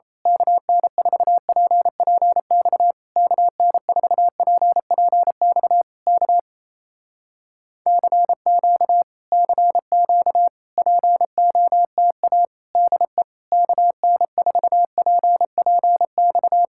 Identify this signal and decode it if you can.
Morse code, heard as a keyed tone, about 27 wpm, 700 Hz. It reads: KN4PPX KN4PPX K CQ CQ POTA DE KN4PPX
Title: Speed-Racing Common QSO Phrases Encore - 28wpm